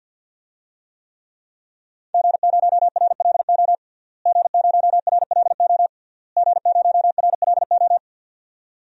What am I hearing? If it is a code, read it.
Morse code, heard as a keyed tone, about 50 wpm, 700 Hz. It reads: G0RLO G0RLO G0RLO
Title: DXCC-100 Call Signs - 5 Characters - Repeated 3x - 50wpm